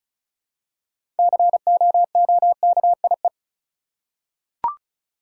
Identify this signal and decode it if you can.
Morse code, heard as a keyed tone, about 36 wpm, 700 Hz. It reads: COOKIE
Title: Six Letter Words - 35wpm